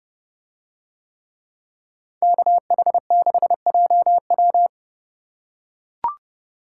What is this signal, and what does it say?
Morse code, heard as a keyed tone, about 30 wpm, 700 Hz. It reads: KH6JW